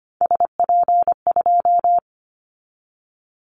Morse code, heard as a keyed tone, about 25 wpm, 700 Hz. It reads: SP2
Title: Callsign Prefixes for Top 25 DXCC - 25wpm